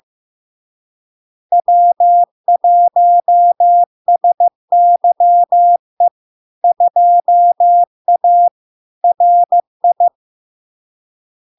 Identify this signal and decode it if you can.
Morse code, heard as a keyed tone, about 15 wpm, 700 Hz. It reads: W1SYE 2A RI
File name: LARK-FD-2022-Log-in-CW-15wpm-700-Hz